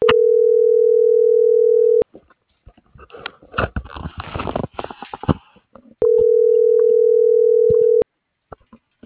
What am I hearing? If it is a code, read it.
Ringback tone (North America), 440 + 480 Hz.
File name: CENTRAL SOTOVA / out alejandro.rodriguez - .55375